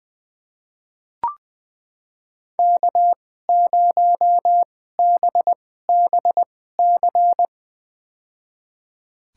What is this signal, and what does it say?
Morse code, heard as a keyed tone, about 20 wpm, spaced as for 14 wpm, 700 Hz. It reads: K0BBC
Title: USA Call Signs - 2x Character Spacing - 20wpm